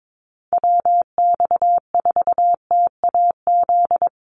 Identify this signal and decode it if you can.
Morse code, heard as a keyed tone, about 22 wpm, 700 Hz. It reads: WX4TAZ
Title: USA Call Signs Repeated 2x 22wpm